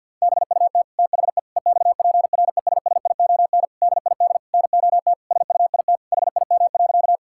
Morse code, heard as a keyed tone, about 50 wpm, 700 Hz. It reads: BUT THE EXPRESSION DID NOT SUIT HIM.
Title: Sentences from Top 1600 Words 50wpm